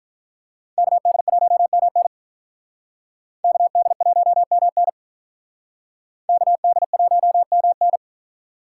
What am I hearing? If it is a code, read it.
Morse code, heard as a keyed tone, about 47 wpm, 700 Hz. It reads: KD1MN KD1MN KD1MN